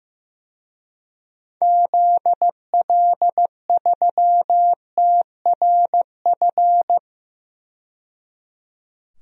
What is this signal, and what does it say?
Morse code, heard as a keyed tone, about 15 wpm, 700 Hz. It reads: ZL3TRF
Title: DXCC-100 Call Signs - 6 Characters - 15wpm